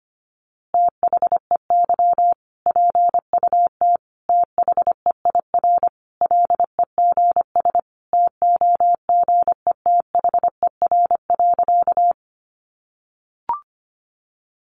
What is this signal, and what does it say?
Morse code, heard as a keyed tone, about 25 wpm, 700 Hz. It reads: THEY PUT THEIR LEGS TOGETHER.
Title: Easy Phrases 25wpm